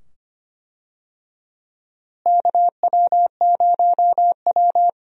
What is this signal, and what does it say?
Morse code, heard as a keyed tone, about 25 wpm, 700 Hz. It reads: KW0W
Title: USA Call Signs - 4 Characters - Repeated 2x - 25wpm